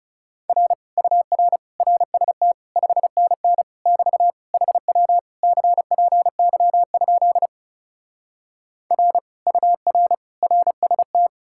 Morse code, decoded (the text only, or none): R UR RST 5NN <BT> HW CPY? R UR RST